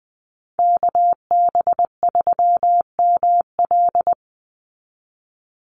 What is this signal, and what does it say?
Morse code, heard as a keyed tone, about 20 wpm, 700 Hz. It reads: KB3ML
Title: Call Signs Repeated 3x - 20wpm